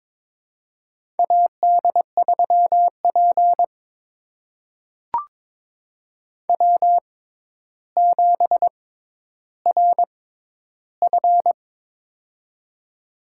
Morse code, heard as a keyed tone, about 22 wpm, spaced as for 11 wpm, 700 Hz. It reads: AD3P W7RF
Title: USA Call Signs - 4 Characters - 6x Character Spacing - 22wpm